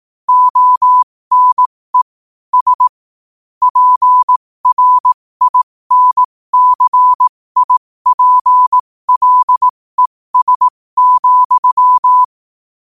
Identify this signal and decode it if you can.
Morse code, heard as a keyed tone, about 18 wpm, spaced as for 15 wpm, 1000 Hz. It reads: ONE S PRINCIPLES,
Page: 15 WPM morse code quotes